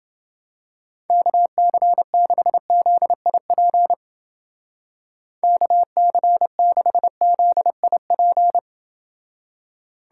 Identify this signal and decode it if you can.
Morse code, heard as a keyed tone, about 29 wpm, 700 Hz. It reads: KC6ZIP KC6ZIP